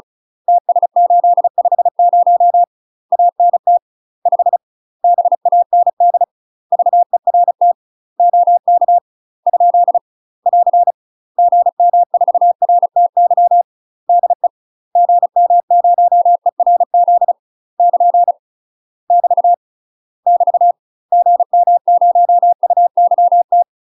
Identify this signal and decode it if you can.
Morse code, heard as a keyed tone, about 34 wpm, 700 Hz. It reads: TS850 ANT 5 BAND VERT OK ? <AR> GM4RTY DE GM0ERZ <KN> <BT> <BT> GM0UYT
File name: UK MASTER QSOs TRAINING / 35WPM / UK MASTER QSOs 0040